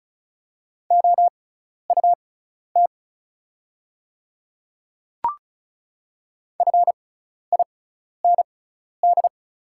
Morse code, heard as a keyed tone, about 36 wpm, spaced as for 12 wpm, 700 Hz. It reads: OUT FIND
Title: 6x Character Spacing - Top 100 Words - 35wpm